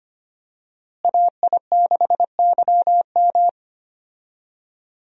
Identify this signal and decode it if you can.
Morse code, heard as a keyed tone, about 25 wpm, 700 Hz. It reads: AI6YM
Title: USA Call Signs - 5 Characters - Repeated 3x - 25wpm